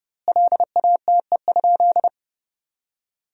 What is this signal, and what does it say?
Morse code, heard as a keyed tone, about 32 wpm, 700 Hz. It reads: LATE?
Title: 2x Word-Spacing - Sentences from Top 300 Words - 30wpm